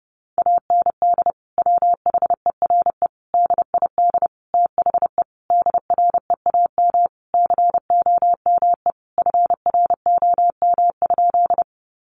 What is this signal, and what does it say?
Morse code, heard as a keyed tone, about 30 wpm, 700 Hz. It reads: AND WHERE DID THE DREAM COME FROM?
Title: Sentences from Top 1500 Words 30wpm